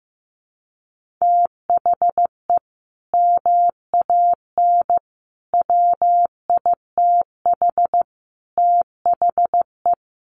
Morse code, heard as a keyed tone, about 15 wpm, 700 Hz. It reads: THE MAN WITH THE
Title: Sentences from Top 1500 Words 15wpm